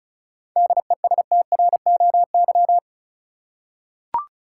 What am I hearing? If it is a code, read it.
Morse code, heard as a keyed tone, about 34 wpm, 700 Hz. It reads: DESTROY
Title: Seven Letter Words - 35wpm